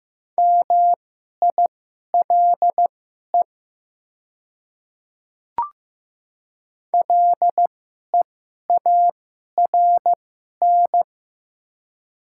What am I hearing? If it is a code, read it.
Morse code, heard as a keyed tone, about 16 wpm, spaced as for 11 wpm, 700 Hz. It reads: MILE LEARN
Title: 2x Character Spacing - Top 300 Words - 15wpm